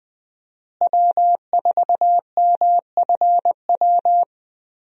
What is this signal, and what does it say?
Morse code, heard as a keyed tone, about 21 wpm, 700 Hz. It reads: W4MFW
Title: USA Call Signs - 5 Characters - Repeated 2x - 20wpm